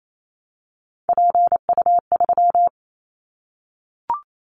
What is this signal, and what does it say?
Morse code, heard as a keyed tone, about 28 wpm, 700 Hz. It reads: PU3
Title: Callsign Prefixes for Top 50 DXCC - 28wpm